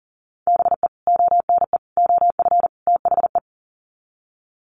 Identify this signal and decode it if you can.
Morse code, heard as a keyed tone, about 40 wpm, 700 Hz. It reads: BE ONE OF THE
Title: Sets of 4 Words 40wpm